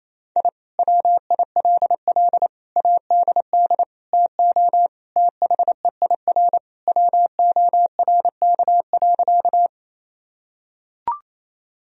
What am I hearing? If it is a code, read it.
Morse code, heard as a keyed tone, about 28 wpm, 700 Hz. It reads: I WILL ADD TO THEIR WORK.
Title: Sentences from Top 1000 Words - Review Part 1 - 28wpm